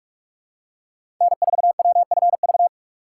Morse code, heard as a keyed tone, about 44 wpm, 690 Hz. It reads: N4WFV
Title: USA Call Signs - 5 Characters - 6x Character Spacing - 45wpm